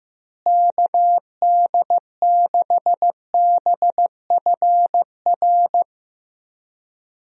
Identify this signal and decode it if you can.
Morse code, heard as a keyed tone, about 15 wpm, 700 Hz. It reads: KD6BFR
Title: USA Call Signs - 6 Characters - Repeated 2x - 15wpm